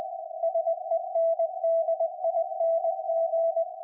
Morse code, heard as a keyed tone, about 20 wpm, 660 Hz. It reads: SENDING
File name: IC-7100